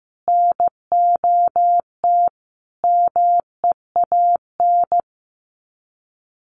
Morse code, heard as a keyed tone, about 15 wpm, 700 Hz. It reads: NOT MEAN
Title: Sets of 5 Words 15wpm